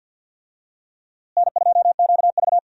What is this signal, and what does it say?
Morse code, heard as a keyed tone, about 50 wpm, 700 Hz. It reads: N2QV
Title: USA Call Signs - 4 Characters - Repeated 2x - 50wpm